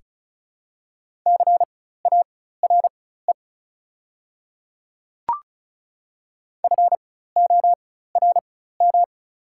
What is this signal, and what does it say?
Morse code, heard as a keyed tone, about 36 wpm, spaced as for 16 wpm, 690 Hz. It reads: CARE FORM
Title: 4x Character Spacing - Top 500 Words - 35wpm